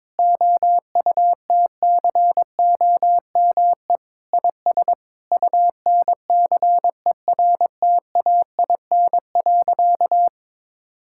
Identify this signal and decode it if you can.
Morse code, heard as a keyed tone, about 21 wpm, 700 Hz. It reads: OUTCOME IS UNCERTAIN.
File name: practice / 22wpm / Sentences with 3-Word Phrases - 22wpm